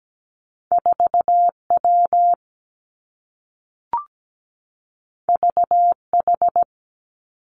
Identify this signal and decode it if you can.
Morse code, heard as a keyed tone, about 17 wpm, 700 Hz. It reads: 4 W V H
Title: Two Challenging Letter-Number Combinations 17wpm